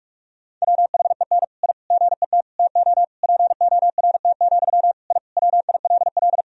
Morse code, heard as a keyed tone, about 46 wpm, 700 Hz. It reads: WHEN I GET TO PORT, I WILL
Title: Sentences from Top 1000 Words - Review Part 1 - 45wpm